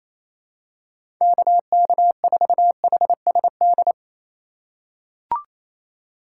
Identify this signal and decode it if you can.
Morse code, heard as a keyed tone, about 27 wpm, 700 Hz. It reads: KK4HSD
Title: USA Call Signs - 2x Character Spacing - 28wpm